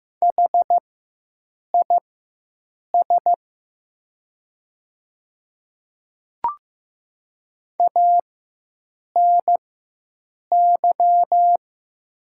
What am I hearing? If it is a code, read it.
Morse code, heard as a keyed tone, about 15 wpm, spaced as for 7 wpm, 700 Hz. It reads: HIS ANY
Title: 4x Character Spacing - Top 200 Words - 15wpm